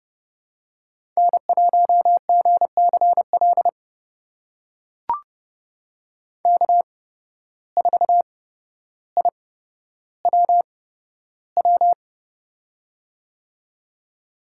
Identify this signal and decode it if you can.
Morse code, heard as a keyed tone, about 31 wpm, spaced as for 13 wpm, 700 Hz. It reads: N1GCL K4IWW
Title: USA Call Signs - 5 Characters - 8x Character Spacing - 30wpm